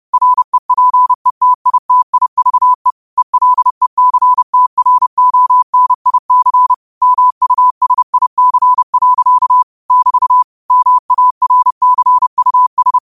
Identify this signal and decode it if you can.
Morse code, heard as a keyed tone, about 30 wpm, 1000 Hz. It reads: REPETITIVE ELECTRONIC MUSIC. <BT> MARCUS
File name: qotdfiles / 30wpm